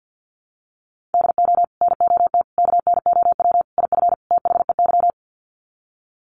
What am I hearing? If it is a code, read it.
Morse code, heard as a keyed tone, about 48 wpm, 700 Hz. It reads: DO NOT KNOW IF THEY
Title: Sets of 5 Words Encore 50wpm